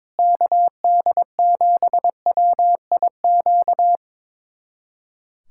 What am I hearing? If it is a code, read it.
Morse code, heard as a keyed tone, about 22 wpm, 700 Hz. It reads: KD7WIQ
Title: USA Callsign DE Callsign - 6 Characters - 22wpm